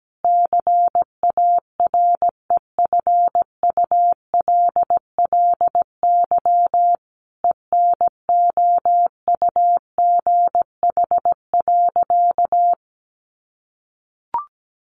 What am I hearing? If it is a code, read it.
Morse code, heard as a keyed tone, about 17 wpm, 700 Hz. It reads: CAREFULLY ENOUGH.
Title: Sentences from Top 1300 Words 17wpm